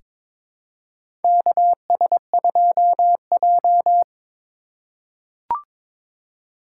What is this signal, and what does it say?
Morse code, heard as a keyed tone, about 22 wpm, 700 Hz. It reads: KS2J
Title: USA Call Signs - 4 Characters - 6x Character Spacing - 22wpm